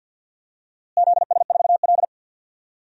Morse code, heard as a keyed tone, about 52 wpm, 700 Hz. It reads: ZS4L
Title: DXCC-100 Call Signs - 4 Characters - Repeated 2x - 50wpm